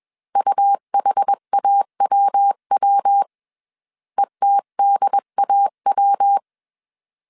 Morse code, heard as a keyed tone, about 21 wpm, 800 Hz. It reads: UHAWW ETDAW